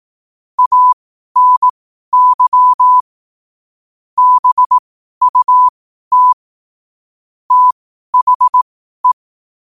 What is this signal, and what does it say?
Morse code, heard as a keyed tone, about 18 wpm, spaced as for 12 wpm, 1000 Hz. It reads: ANY BUT THE
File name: 2025-11-08-12wpm